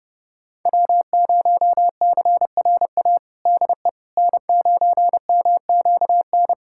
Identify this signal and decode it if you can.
Morse code, heard as a keyed tone, about 32 wpm, 700 Hz. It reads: W0CRA DE N9MQN
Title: USA Callsign DE Callsign - 5 Characters - 30wpm